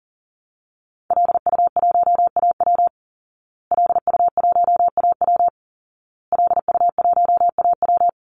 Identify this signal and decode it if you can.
Morse code, heard as a keyed tone, about 40 wpm, 700 Hz. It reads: LU1AW LU1AW LU1AW
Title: Call Signs Repeated 2x - 40wpm